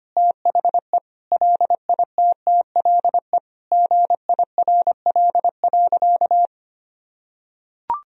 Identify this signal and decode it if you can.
Morse code, heard as a keyed tone, about 25 wpm, 700 Hz. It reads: THE LITTLE GIRL.
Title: Sentences from Top 400 Words and Abbreviations 25wpm